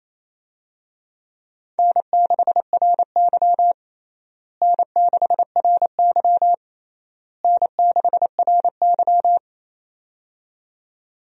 Morse code, heard as a keyed tone, about 28 wpm, 700 Hz. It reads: N6RY N6RY N6RY
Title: USA Call Signs - 4 Characters - Repeated 3x - 28wpm